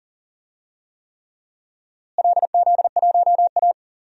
Morse code, heard as a keyed tone, about 40 wpm, 700 Hz. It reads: LZ1A
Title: DXCC-100 Call Signs - 4 Characters - Repeated 3x - 40wpm